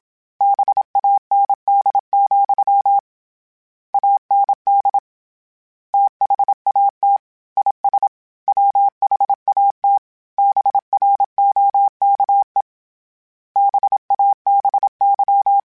Morse code, heard as a keyed tone, about 27 wpm, 800 Hz. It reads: BAND, AND THAT IS WHAT BROKE BABY